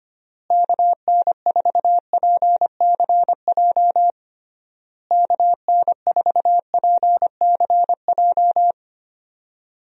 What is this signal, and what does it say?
Morse code, heard as a keyed tone, about 25 wpm, 700 Hz. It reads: KN4PCJ KN4PCJ
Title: USA Call Signs - 6 Characters - Repeated 3x - 25wpm